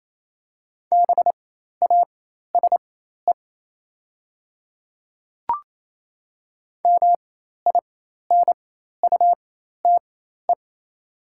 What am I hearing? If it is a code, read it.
Morse code, heard as a keyed tone, about 28 wpm, spaced as for 13 wpm, 690 Hz. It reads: BASE MINUTE
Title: 4x Character Spacing - Top 500 Words - 28wpm